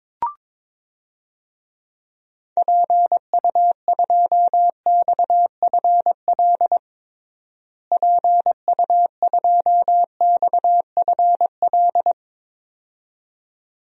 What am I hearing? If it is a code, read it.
Morse code, heard as a keyed tone, about 22 wpm, 700 Hz. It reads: PU2XFL PU2XFL
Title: DXCC-100 Call Signs - 6 Characters - Repeated 2x - 22wpm